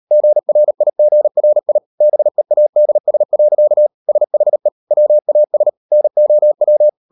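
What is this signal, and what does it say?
Morse code, heard as a keyed tone, about 37 wpm, 600 Hz. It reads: GRIGRI BEADS. SHE WAS NOW
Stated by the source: Morse code practice at 38 WPM, sending Ken Follett's novel Never in English, from CW med Gnister.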